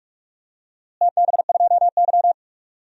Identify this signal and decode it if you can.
Morse code, heard as a keyed tone, about 43 wpm, 700 Hz. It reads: TB2Y
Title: DXCC-100 Call Signs - 4 Characters - Repeated 2x - 45wpm